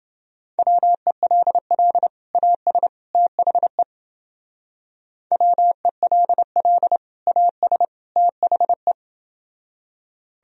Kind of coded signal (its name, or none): Morse code